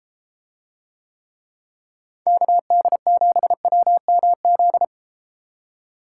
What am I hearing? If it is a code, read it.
Morse code, heard as a keyed tone, about 32 wpm, 700 Hz. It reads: KD7WMZ